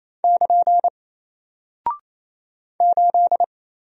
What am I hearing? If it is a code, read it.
Morse code, heard as a keyed tone, about 28 wpm, 700 Hz. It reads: <KN> 8
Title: Single Character (Letter:Number:Punctuation:Prosign) Alt 28wpm